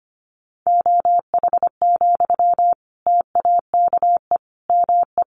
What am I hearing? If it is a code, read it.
Morse code, heard as a keyed tone, about 25 wpm, 700 Hz. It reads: OH, TAKE ME
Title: Sentences from Top 1400 Words 25wpm